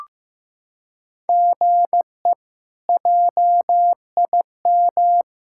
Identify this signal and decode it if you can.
Morse code, heard as a keyed tone, about 15 wpm, 700 Hz. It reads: GE JIM
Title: K1USN SST QSO Practice - 15wpm